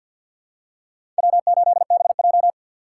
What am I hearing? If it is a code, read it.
Morse code, heard as a keyed tone, about 51 wpm, 700 Hz. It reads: W8BJ
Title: USA Call Signs - 4 Characters - 3x Character Spacing - 50wpm